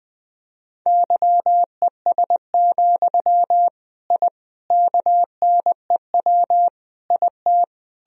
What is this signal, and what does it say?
Morse code, heard as a keyed tone, about 21 wpm, 700 Hz. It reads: YES, I KNEW IT
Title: Sentences from Top 1000 Words - Review Part 2 - 20wpm